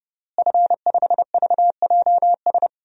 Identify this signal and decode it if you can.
Morse code, heard as a keyed tone, about 30 wpm, 700 Hz. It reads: F5VJS
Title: DXCC-100 Call Signs - 5 Characters - Repeated 2x - 30wpm